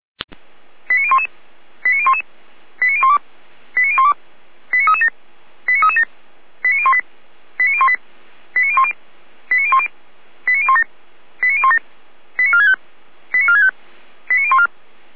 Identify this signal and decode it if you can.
Five-tone selective call (ZVEI), heard as quick runs of five tones, tones of about 70 ms. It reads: 89011 89011 89012 89012 89337 89337 89018 89018 89010 89010 89017 89017 89465 89465 89014